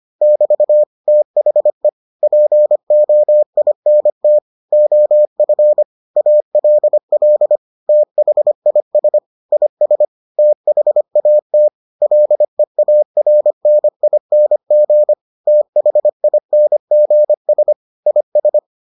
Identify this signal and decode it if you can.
Morse code, heard as a keyed tone, about 25 wpm, 600 Hz. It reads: <BT> THE POINT OF ALL THIS IS THAT LEARNING THINGS IS